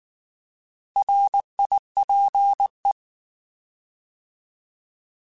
Morse code, heard as a keyed tone, about 19 wpm, 800 Hz. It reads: RIPE
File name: Alphabetical / Alphabetical mp3s / Rs @18